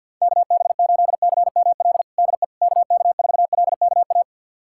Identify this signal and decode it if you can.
Morse code, heard as a keyed tone, about 51 wpm, 700 Hz. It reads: KB8CML DE KK4LKA
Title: USA Callsign DE Callsign 50wpm